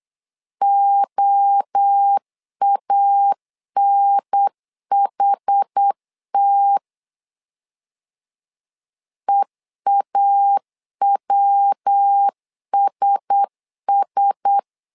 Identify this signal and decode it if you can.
Morse code, heard as a keyed tone, about 8 wpm, 800 Hz. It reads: OANHT EAWSS
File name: LEZIONI / 4 Q1S 8wpm 2710